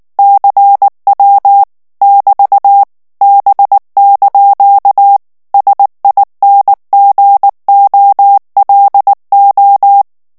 Morse code, heard as a keyed tone, about 19 wpm, 800 Hz. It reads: CW <BT> B) SINGOLO
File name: morse mp3 / ITALIEN texte clair 2 vitesse 18 mots minute